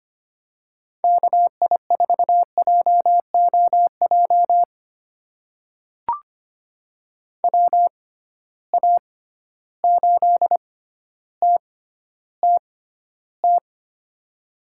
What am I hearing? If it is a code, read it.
Morse code, heard as a keyed tone, about 25 wpm, spaced as for 13 wpm, 700 Hz. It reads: KI4JOJ WA8TTT
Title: USA Call Signs - 6 Characters - 6x Character Spacing - 25wpm